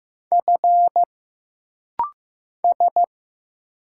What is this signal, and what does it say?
Morse code, heard as a keyed tone, about 15 wpm, 700 Hz. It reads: F S
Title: Single Letters - Rapid-Fire 15wpm